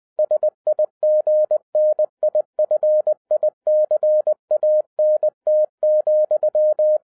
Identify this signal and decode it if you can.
Morse code, heard as a keyed tone, about 20 wpm, 610 Hz. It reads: SIGNIFICANT,
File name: 2011-07-mp3 / n08 A LOT HAS CHANGED 20wpm